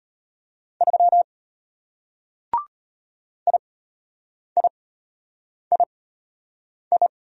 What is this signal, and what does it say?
Morse code, heard as a keyed tone, about 33 wpm, spaced as for 8 wpm, 700 Hz. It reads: 3 IIII